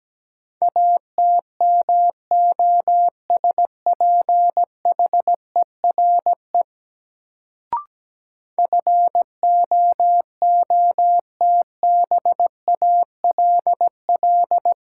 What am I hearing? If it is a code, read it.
Morse code, heard as a keyed tone, about 17 wpm, 700 Hz. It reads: ATMOSPHERE FOOTBALL
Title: Sentences from Top 1600 Words - New Word Drill - 17wpm